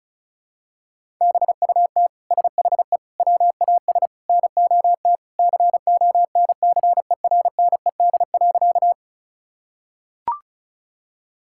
Morse code, heard as a keyed tone, about 36 wpm, 700 Hz. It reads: BUT SHE WAS NOT CONCERNED.
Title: Sentences from Top 1000 Words - Review Part 6 - 35wpm